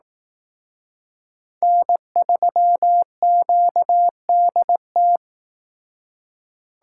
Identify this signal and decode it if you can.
Morse code, heard as a keyed tone, about 18 wpm, 700 Hz. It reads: N3QDT
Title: Speed Racing - USA 1x3 Call Signs 15wpm